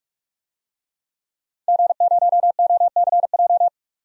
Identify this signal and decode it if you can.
Morse code, heard as a keyed tone, about 46 wpm, 700 Hz. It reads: G0OCJ
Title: DXCC-100 Call Signs - 5 Characters - 45wpm